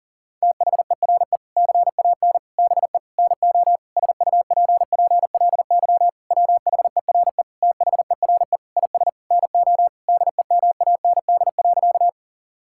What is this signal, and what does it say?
Morse code, heard as a keyed tone, about 40 wpm, 700 Hz. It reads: THERE CAN BE NO SUPPLY WHERE THERE IS NO DEMAND.
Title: Sentences from Top 1000 Words - Review Part 5 - 40wpm